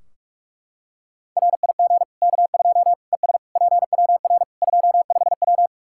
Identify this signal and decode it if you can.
Morse code, heard as a keyed tone, about 41 wpm, 700 Hz. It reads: RIG K2 ES PWR 25W
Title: Practice QSO - Exchange 2 - Info - 45wpm